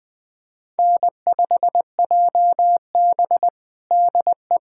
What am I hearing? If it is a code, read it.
Morse code, heard as a keyed tone, about 20 wpm, 700 Hz. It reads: N5JB DE
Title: USA Callsign DE Callsign - 4 Characters - 20wpm